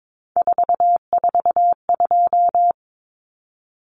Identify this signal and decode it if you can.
Morse code, heard as a keyed tone, about 22 wpm, 700 Hz. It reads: 442
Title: Numbers - 3-Digit 22wpm